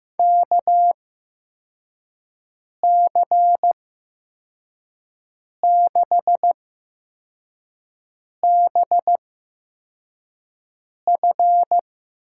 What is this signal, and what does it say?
Morse code, heard as a keyed tone, about 15 wpm, spaced as for 4 wpm, 700 Hz. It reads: KC6BF
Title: USA Call Signs - 6 Characters - 8x Character Spacing - 15wpm